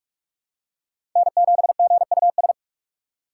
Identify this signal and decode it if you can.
Morse code, heard as a keyed tone, about 44 wpm, 700 Hz. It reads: N7GUS
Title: USA Call Signs - 4x Character Spacing - 45wpm